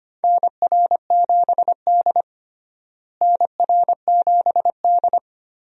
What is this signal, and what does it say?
Morse code, heard as a keyed tone, about 25 wpm, 700 Hz. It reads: NR7D NR7D
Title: USA Call Signs - 4 Characters - Repeated 3x - 25wpm